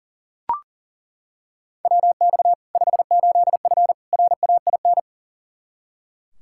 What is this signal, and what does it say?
Morse code, heard as a keyed tone, about 40 wpm, 700 Hz. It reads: WX 58F RAIN
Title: Practice QSO - Exchange 2 - Info - 40wpm